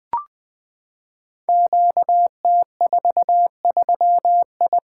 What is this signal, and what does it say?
Morse code, heard as a keyed tone, about 21 wpm, 700 Hz. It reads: QT43I
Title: Code Groups - 5 Letter:Numbers - 20wpm